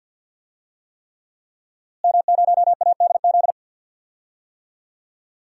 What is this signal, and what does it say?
Morse code, heard as a keyed tone, about 47 wpm, 700 Hz. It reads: M0ADZ
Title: DXCC-100 Call Signs - 5 Characters - Repeated 3x - 50wpm